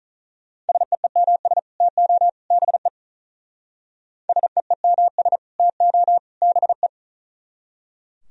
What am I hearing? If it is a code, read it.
Morse code, heard as a keyed tone, about 43 wpm, 700 Hz. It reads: SEEMS TO BE SEEMS TO BE
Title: Speed-Racing - Sets of 3 Words 35wpm